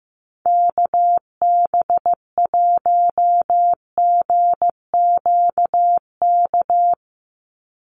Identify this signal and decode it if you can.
Morse code, heard as a keyed tone, about 15 wpm, 700 Hz. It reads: KB1GQK
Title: USA Call Signs 15wpm